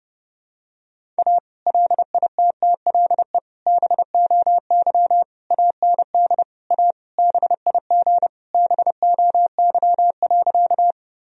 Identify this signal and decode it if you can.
Morse code, heard as a keyed tone, about 32 wpm, 700 Hz. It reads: A LITTLE BOY AND A BIG BOY.
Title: Sentences from Top 1000 Words - Review Part 1 - 30wpm